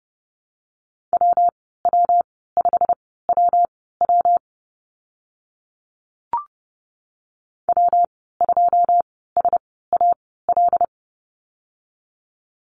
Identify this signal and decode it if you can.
Morse code, heard as a keyed tone, about 30 wpm, spaced as for 17 wpm, 700 Hz. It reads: WW5WW W2SAL
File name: Lesson 19g - Letter w - Call Signs 30wpm